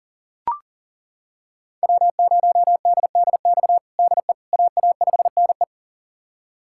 Morse code, heard as a keyed tone, about 40 wpm, 700 Hz. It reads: W0DDX DE AA5NE
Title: USA Callsign DE Callsign - 5 Characters - 40wpm